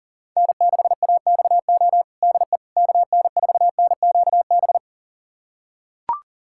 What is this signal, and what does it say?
Morse code, heard as a keyed tone, about 40 wpm, 700 Hz. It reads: N6AXO DE KN4NQB
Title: USA Callsign DE Callsign 40wpm